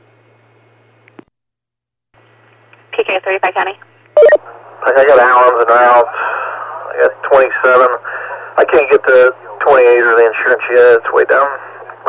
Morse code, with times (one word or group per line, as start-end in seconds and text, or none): none